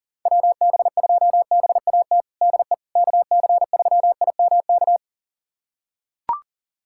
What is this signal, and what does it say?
Morse code, heard as a keyed tone, about 40 wpm, 700 Hz. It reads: WB2BAT DE KC3IMK
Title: USA Callsign DE Callsign - 6 Characters - 40wpm